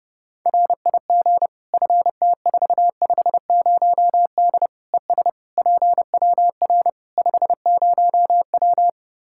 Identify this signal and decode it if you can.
Morse code, heard as a keyed tone, about 30 wpm, 700 Hz. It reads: RIG FT450D ES PWR 50W
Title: K1USN SST QSO Practice 30wpm